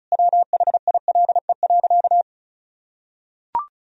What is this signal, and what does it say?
Morse code, heard as a keyed tone, about 36 wpm, 700 Hz. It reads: WHILE.
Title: Sentences from Top 1000 Words - Review Part 1 - 35wpm